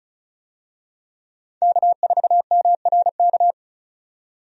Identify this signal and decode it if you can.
Morse code, heard as a keyed tone, about 34 wpm, 700 Hz. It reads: K4MRK
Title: USA Call Signs - 5 Characters - 35wpm